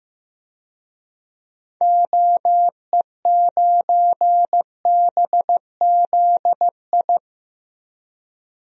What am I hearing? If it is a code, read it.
Morse code, heard as a keyed tone, about 15 wpm, 700 Hz. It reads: OE9BZI
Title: DXCC-100 Call Signs - 6 Characters - 15wpm